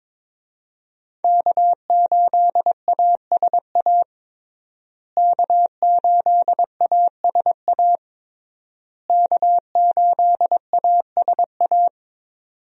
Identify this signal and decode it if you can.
Morse code, heard as a keyed tone, about 22 wpm, 700 Hz. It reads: K8ASA K8ASA K8ASA
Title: USA Call Signs - 5 Characters - Repeated 3x - 22wpm